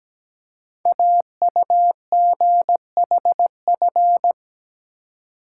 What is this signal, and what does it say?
Morse code, heard as a keyed tone, about 17 wpm, 700 Hz. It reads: AUGHF
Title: Code Groups - 5 Letters - 17wpm